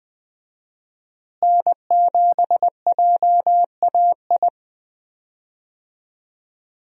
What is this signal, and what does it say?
Morse code, heard as a keyed tone, about 20 wpm, 700 Hz. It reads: N7JAI
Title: USA Call Signs - 5 Characters - Repeated 3x - 20wpm